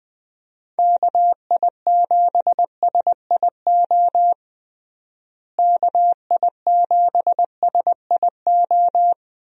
Morse code, heard as a keyed tone, about 20 wpm, 700 Hz. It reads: KI7SIO KI7SIO
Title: USA Call Signs - 6 Characters - Repeated 2x - 20wpm